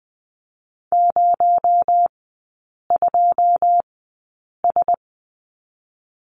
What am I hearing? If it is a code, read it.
Morse code, heard as a keyed tone, about 20 wpm, spaced as for 8 wpm, 700 Hz. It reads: 02S
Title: 218 Lesson 35e - Number 0 - Three Characters 20wpm